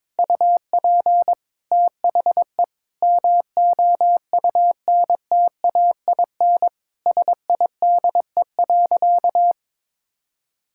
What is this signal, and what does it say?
Morse code, heard as a keyed tone, about 21 wpm, 700 Hz. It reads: UP THE MOUNTAIN SIDE.
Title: Sentences from Top 1000 Words - Review Part 6 - 22wpm